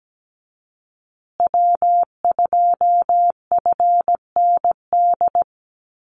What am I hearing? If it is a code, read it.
Morse code, heard as a keyed tone, about 17 wpm, 700 Hz. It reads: W2FND
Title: USA Call Signs Repeated 2x 17wpm